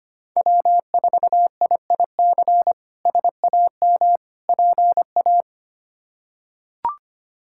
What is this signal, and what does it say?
Morse code, heard as a keyed tone, about 25 wpm, 700 Hz. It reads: W4IIC SAM PA
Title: K1USN SST QSO Practice - 25wpm